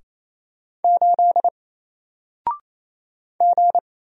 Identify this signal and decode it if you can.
Morse code, heard as a keyed tone, about 28 wpm, 700 Hz. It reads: 8 G
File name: Single-Letter-Number-28wpm